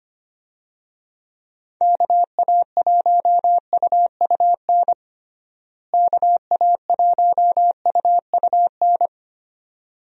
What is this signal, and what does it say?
Morse code, heard as a keyed tone, about 25 wpm, 700 Hz. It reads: KA1UUN KA1UUN
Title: USA Call Signs - 6 Characters - Repeated 2x - 25wpm